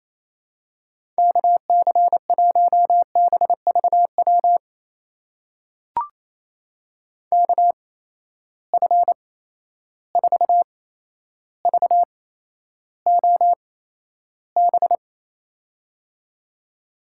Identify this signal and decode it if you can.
Morse code, heard as a keyed tone, about 28 wpm, spaced as for 12 wpm, 700 Hz. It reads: KC1BVW KF4VOB
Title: USA Call Signs - 6 Characters - 8x Character Spacing - 28wpm